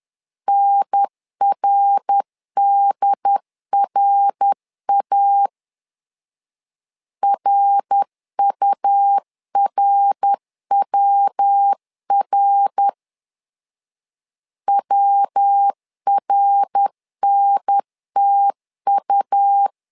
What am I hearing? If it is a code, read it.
Morse code, heard as a keyed tone, about 10 wpm, 800 Hz. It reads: NRDRA RURWR WRNTU
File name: LEZIONI / 3 MWRO / 3 MWR 10wpm 1952